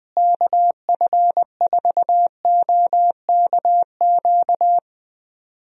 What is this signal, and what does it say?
Morse code, heard as a keyed tone, about 20 wpm, 700 Hz. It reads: KF4OKQ
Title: USA Call Signs - 6 Characters - Repeated 3x - 20wpm